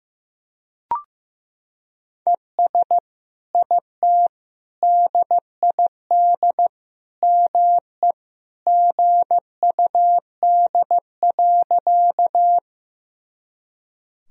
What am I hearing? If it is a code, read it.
Morse code, heard as a keyed tone, about 15 wpm, 700 Hz. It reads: ES IT DID ME GUD.
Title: Sentences from Top 200 Words and Abbreviations 15wpm